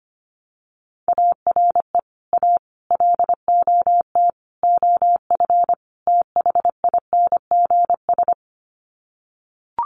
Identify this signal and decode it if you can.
Morse code, heard as a keyed tone, about 25 wpm, 700 Hz. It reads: ARE A LOT OF THINGS
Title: Sets of 5 Words Encore2 25wpm